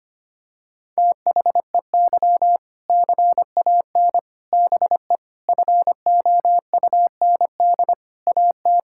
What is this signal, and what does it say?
Morse code, heard as a keyed tone, about 25 wpm, 700 Hz. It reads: THEY CAN BE FOUND AT
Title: Sentences from Top 1600 Words 25wpm